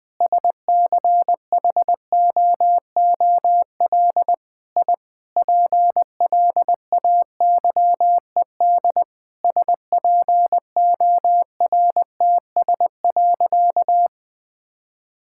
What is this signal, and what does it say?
Morse code, heard as a keyed tone, about 20 wpm, 700 Hz. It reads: SCHOOL I PLAYED SPORTS.
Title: Sentences with 6-Word Phrases - Spoken Beginning - 20wpm